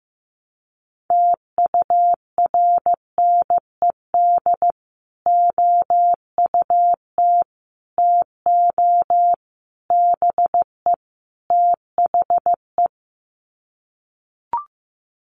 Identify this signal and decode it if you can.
Morse code, heard as a keyed tone, about 15 wpm, 700 Hz. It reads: TURNED OUT TO BE THE
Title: Sets of 5 Words Encore 15wpm